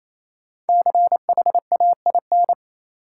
Morse code, heard as a keyed tone, about 28 wpm, 700 Hz. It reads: CHAIN
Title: Sentences from Top 1500 Words - New Word Drill Alt - 28wpm